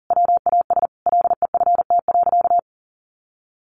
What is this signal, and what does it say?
Morse code, heard as a keyed tone, about 37 wpm, 700 Hz. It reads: WAS LEFT.
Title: Sentences from Top 1200 Words 40wpm